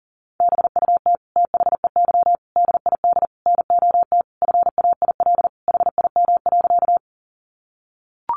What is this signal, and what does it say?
Morse code, heard as a keyed tone, about 40 wpm, 700 Hz. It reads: BUT THEY DID NOT FAIL HIM.
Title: Sentences from Top 1100 Words 40wpm